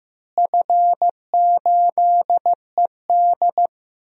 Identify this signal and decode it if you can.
Morse code, heard as a keyed tone, about 15 wpm, 700 Hz. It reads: F8ED
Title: DXCC-100 Call Signs - 4 Characters - Repeated 3x - 15wpm